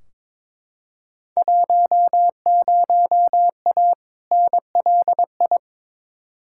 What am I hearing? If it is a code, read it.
Morse code, heard as a keyed tone, about 22 wpm, 700 Hz. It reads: 10A NLI
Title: Field Day Exchanges Encore - 22wpm